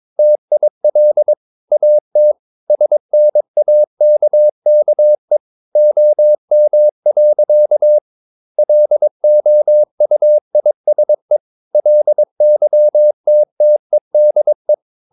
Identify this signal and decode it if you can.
Morse code, heard as a keyed tone, about 22 wpm, 600 Hz. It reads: TIL AT SNAKKE OM. LOUISE LYTTEDE
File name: Ferie QRS 22wpm